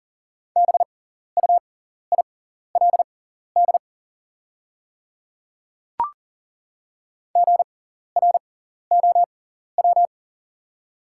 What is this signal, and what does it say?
Morse code, heard as a keyed tone, about 38 wpm, spaced as for 14 wpm, 690 Hz. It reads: BUILD GROW
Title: 6x Character Spacing - Top 300 Words - 40wpm